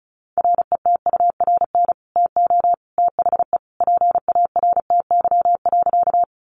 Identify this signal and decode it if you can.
Morse code, heard as a keyed tone, about 34 wpm, 700 Hz. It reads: RETURN TO THE PARTY.
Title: Sentences from Top 1400 Words 35wpm